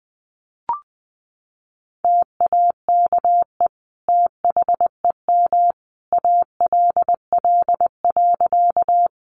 Morse code, heard as a keyed tone, about 20 wpm, 700 Hz. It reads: TAKE THEM ALL.
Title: Sentences from Top 200 Words 20wpm